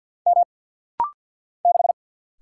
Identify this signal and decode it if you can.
Morse code, heard as a keyed tone, about 50 wpm, 690 Hz. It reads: M 6
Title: Single Letter-Number - Rapid-Fire 50wpm